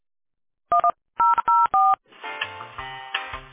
DTMF, heard as telephone touch tones, tones of about 210 ms. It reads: 1004